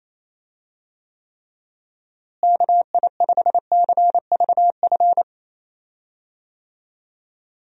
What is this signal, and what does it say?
Morse code, heard as a keyed tone, about 28 wpm, 700 Hz. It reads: KI5CVF
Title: USA Call Signs - 6 Characters - Repeated 2x - 28wpm